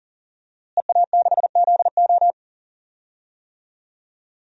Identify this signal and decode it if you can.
Morse code, heard as a keyed tone, about 40 wpm, 700 Hz. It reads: EA6ZO
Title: DXCC-100 Call Signs - 5 Characters - Repeated 3x - 40wpm